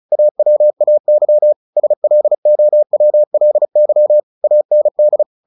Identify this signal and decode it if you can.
Morse code, heard as a keyed tone, about 34 wpm, 600 Hz. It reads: AWAY SLOWLY AND
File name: the old man and the sea / the old man and the sea 1 35wpm